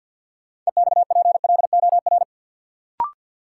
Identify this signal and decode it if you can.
Morse code, heard as a keyed tone, about 53 wpm, 700 Hz. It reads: EXPLOR
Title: Sentences from Top 1100 Words - New Word Drill Alt - 50wpm